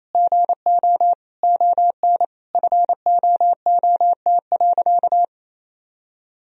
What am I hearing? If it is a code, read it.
Morse code, heard as a keyed tone, about 28 wpm, 700 Hz. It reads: GO ON FOOT.
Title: Sentences from Top 1000 Words - Review Part 3 - 28wpm